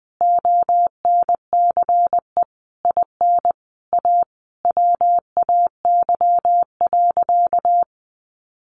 Morse code, heard as a keyed tone, about 20 wpm, 700 Hz. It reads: ONCE IN A WAY.
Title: Sentences from Top 500 Words 20wpm